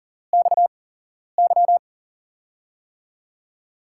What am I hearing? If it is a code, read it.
Morse code, heard as a keyed tone, about 38 wpm, spaced as for 11 wpm, 700 Hz. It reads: XY
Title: USA Call Signs - 8x Character Spacing - 40wpm